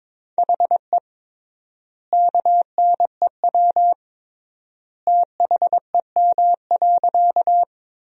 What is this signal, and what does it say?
Morse code, heard as a keyed tone, about 22 wpm, 700 Hz. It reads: HE KNEW THEM.
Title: 3x Word-Spacing - Sentences from Top 400 Words - 22wpm